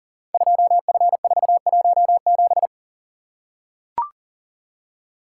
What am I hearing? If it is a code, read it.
Morse code, heard as a keyed tone, about 40 wpm, 700 Hz. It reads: 2F417
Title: Code Groups - 5 Letter:Numbers - 40wpm